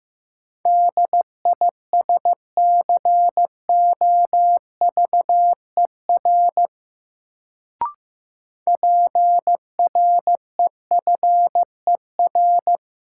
Morse code, heard as a keyed tone, about 15 wpm, 700 Hz. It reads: DISCOVER PREFER
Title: Sentences from Top 1100 Words - New Word Drill - 15wpm